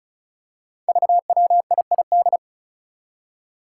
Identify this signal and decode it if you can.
Morse code, heard as a keyed tone, about 36 wpm, 700 Hz. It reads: VWIID
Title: Five Challenging Letter-Number Combinations 35wpm